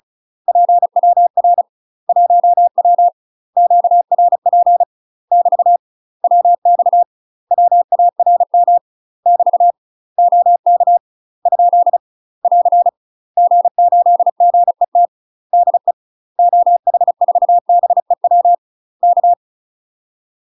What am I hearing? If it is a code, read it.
Morse code, heard as a keyed tone, about 36 wpm, 700 Hz. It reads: PWR 1W QRP <BT> WX WARM <BT> OK ? <AR> G8GET DE OH4BEW K
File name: UK MASTER QSOs TRAINING / 35WPM / UK MASTER QSOs 0014